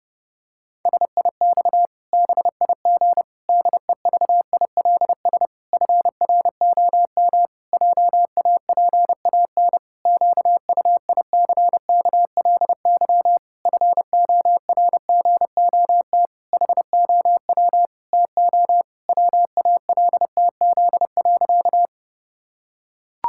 Morse code, heard as a keyed tone, about 30 wpm, 700 Hz. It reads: SIX BIG DEVILS FROM JAPAN QUICKLY FORGOT HOW TO WALTZ.
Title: Pangram Challenge v2 - 30wpm